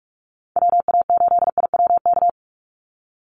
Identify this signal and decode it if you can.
Morse code, heard as a keyed tone, about 44 wpm, 700 Hz. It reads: WA8IWK
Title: Call Signs - 45wpm